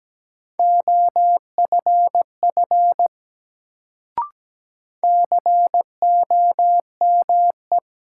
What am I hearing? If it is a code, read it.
Morse code, heard as a keyed tone, about 17 wpm, 700 Hz. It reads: OFF COME
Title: Sets of 2 Words - New Word Drill - 17wpm